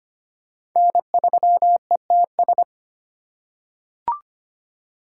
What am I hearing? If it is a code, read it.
Morse code, heard as a keyed tone, about 25 wpm, 700 Hz. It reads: N3ETS
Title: USA Call Signs - 5 Characters - 3x Character Spacing - 25wpm